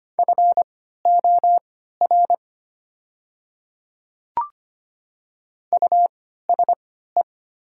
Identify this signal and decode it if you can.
Morse code, heard as a keyed tone, about 25 wpm, spaced as for 14 wpm, 700 Hz. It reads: FOR USE
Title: 3x Character Spacing - Top 100 Words - 25wpm